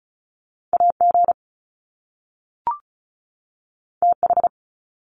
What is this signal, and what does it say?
Morse code, heard as a keyed tone, about 36 wpm, 700 Hz. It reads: AG TH
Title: Top Two Letter Combinations in Words 35wpm